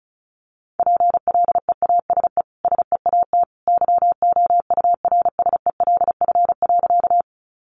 Morse code, heard as a keyed tone, about 34 wpm, 700 Hz. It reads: PLEASE SEAT YOURSELF.
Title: Sentences from Top 1500 Words 35wpm